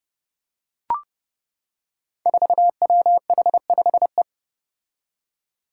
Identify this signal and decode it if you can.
Morse code, heard as a keyed tone, about 30 wpm, 700 Hz. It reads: 4WH5E
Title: Five Challenging Letter-Number Combinations 30wpm